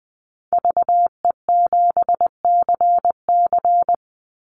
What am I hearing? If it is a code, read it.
Morse code, heard as a keyed tone, about 20 wpm, 700 Hz. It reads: VE7CC
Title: Call Signs Repeated 2x - 20wpm